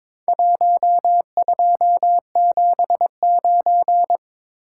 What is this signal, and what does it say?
Morse code, heard as a keyed tone, about 23 wpm, 700 Hz. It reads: 1279
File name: practice / 22wpm / Four Letter-Number Combinations - 22wpm